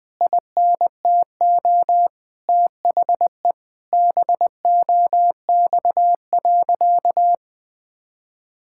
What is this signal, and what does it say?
Morse code, heard as a keyed tone, about 20 wpm, 700 Hz. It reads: INTO THE BOX.
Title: Sentences from Top 1000 Words - Review Part 6 - 20wpm